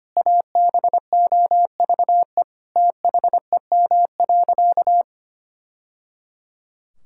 Morse code, heard as a keyed tone, about 25 wpm, 700 Hz. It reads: ABOVE THEM.
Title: Sentences from Top 1000 Words - Review Part 5 - 25wpm